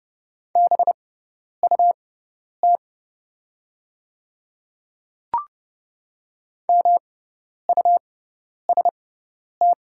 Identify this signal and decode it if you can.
Morse code, heard as a keyed tone, about 30 wpm, spaced as for 10 wpm, 700 Hz. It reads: BUT MUST
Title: 6x Character Spacing - Top 200 Words - 30wpm